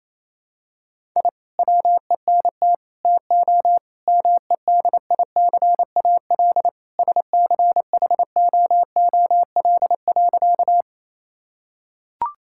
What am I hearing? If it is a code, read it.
Morse code, heard as a keyed tone, about 28 wpm, 700 Hz. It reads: I WENT TO MEDICAL SCHOOL.
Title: Sentences from Top 1000 Words - Review Part 5 - 28wpm